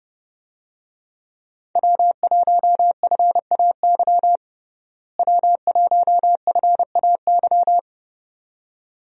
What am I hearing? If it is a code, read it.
Morse code, heard as a keyed tone, about 32 wpm, 700 Hz. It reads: W1FAY W1FAY
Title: USA Call Signs - 5 Characters - Repeated 2x - 30wpm